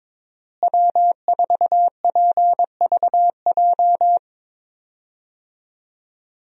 Morse code, heard as a keyed tone, about 22 wpm, 700 Hz. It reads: W4PVJ
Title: USA Call Signs - 5 Characters - Repeated 2x - 22wpm